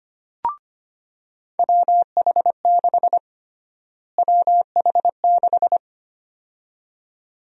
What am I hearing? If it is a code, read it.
Morse code, heard as a keyed tone, about 25 wpm, 700 Hz. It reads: WH6 WH6
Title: USA Call Sign Prefixes - Repeated 2x - 25wpm